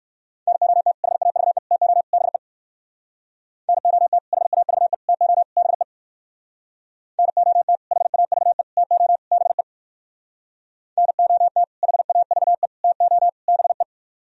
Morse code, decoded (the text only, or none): NOT HAVE TO BE NOT HAVE TO BE NOT HAVE TO BE NOT HAVE TO BE